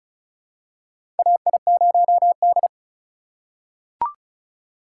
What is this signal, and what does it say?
Morse code, heard as a keyed tone, about 36 wpm, 700 Hz. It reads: AI0D
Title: USA Call Signs - 4 Characters - 4x Character Spacing - 35wpm